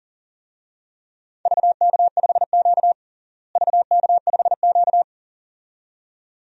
Morse code, heard as a keyed tone, about 37 wpm, 700 Hz. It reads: VK5Q VK5Q
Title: DXCC-100 Call Signs - 4 Characters - Repeated 2x - 40wpm